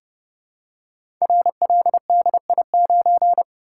Morse code, heard as a keyed tone, about 30 wpm, 700 Hz. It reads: RLDI9
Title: Code Groups - 5 Letter:Numbers - 30wpm